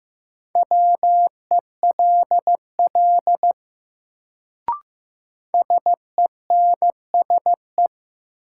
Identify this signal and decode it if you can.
Morse code, heard as a keyed tone, about 15 wpm, 700 Hz. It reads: WELL SENSE
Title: Sets of 2 Words - New Word Drill - 15wpm